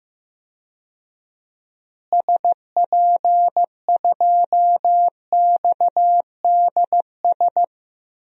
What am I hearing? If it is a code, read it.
Morse code, heard as a keyed tone, about 15 wpm, 700 Hz. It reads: SP2XDS
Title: DXCC-100 Call Signs - 6 Characters - Repeated 2x - 15wpm